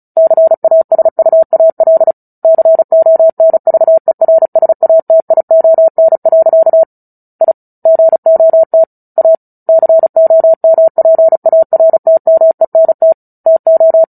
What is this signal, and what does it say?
Morse code, heard as a keyed tone, about 34 wpm, 650 Hz. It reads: CASUAL CONVERSATION. I GOT A COMPARTMENT TO
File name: Book Two / 09-Chapter 9 - 35 WPM